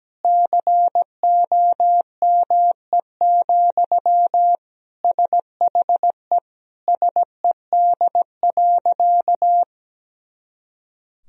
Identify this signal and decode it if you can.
Morse code, heard as a keyed tone, about 17 wpm, 700 Hz. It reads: COME, SHE SED.
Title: Sentences from Top 100 Words and Abbreviations 17wpm